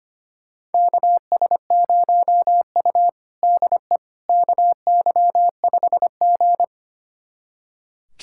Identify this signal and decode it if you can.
Morse code, heard as a keyed tone, about 25 wpm, 700 Hz. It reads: KS0U DE KY5G
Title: USA Callsign DE Callsign - 4 Characters - 25wpm